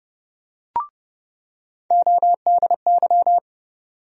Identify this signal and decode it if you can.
Morse code, heard as a keyed tone, about 31 wpm, 700 Hz. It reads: ODY
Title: USA Call Sign Postfixes - 3 Letters 30wpm